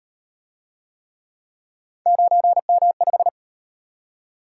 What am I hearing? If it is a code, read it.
Morse code, heard as a keyed tone, about 35 wpm, 700 Hz. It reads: 9M5